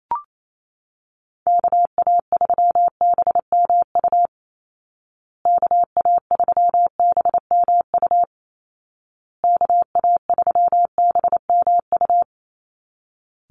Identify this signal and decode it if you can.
Morse code, heard as a keyed tone, about 28 wpm, 700 Hz. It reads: KA3BMU KA3BMU KA3BMU
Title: Call Signs Repeated 2x - 28wpm